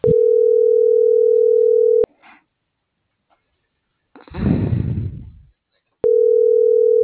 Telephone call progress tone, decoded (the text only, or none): ringback tone